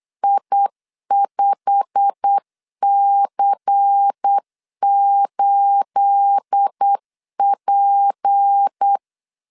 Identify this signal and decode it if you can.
Morse code, heard as a keyed tone, about 8 wpm, 800 Hz. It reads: I5C8P
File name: LEZIONI / 10 SIGNS / 10B INT 8wpm 3534